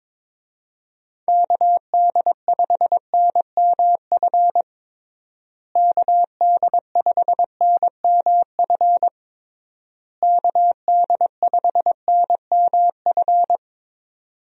Morse code, heard as a keyed tone, about 21 wpm, 700 Hz. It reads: KD5NMF KD5NMF KD5NMF
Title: USA Call Signs - 6 Characters - Repeated 3x - 22wpm